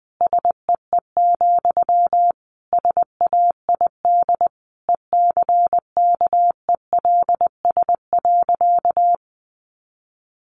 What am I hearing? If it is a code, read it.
Morse code, heard as a keyed tone, about 20 wpm, 700 Hz. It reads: SEE, SAID ECKELS.
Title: A Sound Of Thunder 20wpm